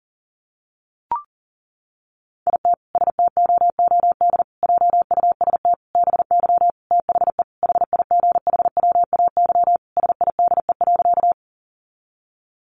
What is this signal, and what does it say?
Morse code, heard as a keyed tone, about 40 wpm, 700 Hz. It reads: IT STOOD JUST BY THE HIGHWAY SIDE.
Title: Sentences from Top 1400 Words 40wpm